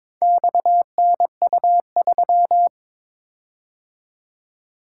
Morse code, heard as a keyed tone, about 22 wpm, 700 Hz. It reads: XNU3
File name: practice / 22wpm / Four Letter-Number Combinations - 22wpm